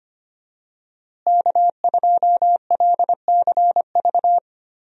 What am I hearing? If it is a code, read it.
Morse code, heard as a keyed tone, about 25 wpm, 700 Hz. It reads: K2LCV
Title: USA Call Signs - 5 Characters - Repeated 2x - 25wpm